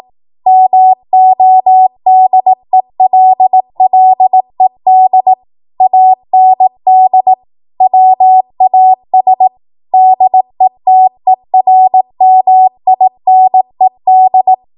Morse code, heard as a keyed tone, about 18 wpm, 750 Hz. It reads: MODELLED AND WAS DETERMINED
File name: Morse / 18WPM